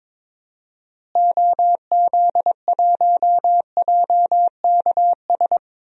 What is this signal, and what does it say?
Morse code, heard as a keyed tone, about 22 wpm, 700 Hz. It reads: OZ1JKS
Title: DXCC-100 Call Signs - 6 Characters - Repeated 2x - 22wpm